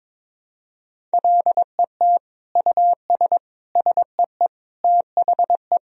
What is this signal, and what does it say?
Morse code, heard as a keyed tone, about 21 wpm, 700 Hz. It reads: LET US SEE THE
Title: Sentences from Top 1000 Words - Review Part 3 - 22wpm